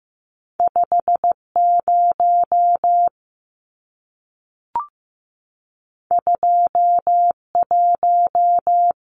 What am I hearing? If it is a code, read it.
Morse code, heard as a keyed tone, about 15 wpm, 700 Hz. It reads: 50 21
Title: Numbers - 2-Digit 15wpm